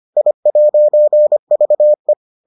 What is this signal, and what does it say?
Morse code, heard as a keyed tone, about 25 wpm, 600 Hz. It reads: I'VE